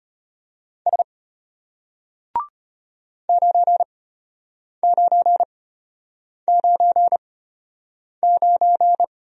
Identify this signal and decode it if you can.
Morse code, heard as a keyed tone, about 33 wpm, spaced as for 8 wpm, 700 Hz. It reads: S 9999